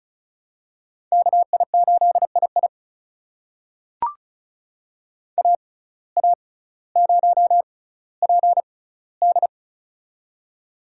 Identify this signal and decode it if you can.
Morse code, heard as a keyed tone, about 36 wpm, spaced as for 18 wpm, 700 Hz. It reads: KI8II AA0PD
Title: USA Call Signs - 5 Characters - 6x Character Spacing - 35wpm